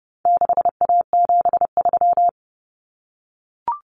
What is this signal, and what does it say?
Morse code, heard as a keyed tone, about 30 wpm, 700 Hz. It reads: 6A73
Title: Common Cut Numbers - 4-Digit 30wpm